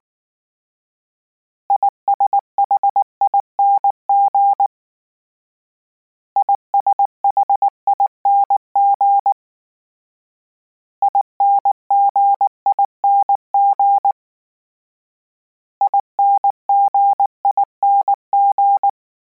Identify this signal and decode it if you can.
Morse code, heard as a keyed tone, about 19 wpm, 800 Hz. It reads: ISHING ISHING INGING INGING